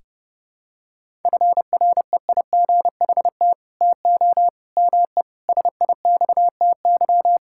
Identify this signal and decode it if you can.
Morse code, heard as a keyed tone, about 32 wpm, 700 Hz. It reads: FREIGHT TO ME SIXTY
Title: Pangram Challenge v2 - 30wpm